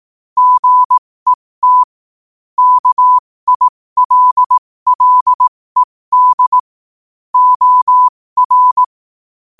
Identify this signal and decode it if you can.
Morse code, heard as a keyed tone, about 18 wpm, spaced as for 15 wpm, 1000 Hz. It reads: GET KILLED OR
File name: qotdfiles / 15wpm